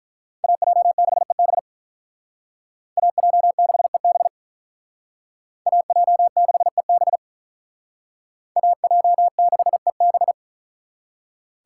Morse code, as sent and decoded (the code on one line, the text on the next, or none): .- .--- -.... -... / .- .--- -.... -... / .- .--- -.... -... / .- .--- -.... -...
AJ6EB AJ6EB AJ6EB AJ6EB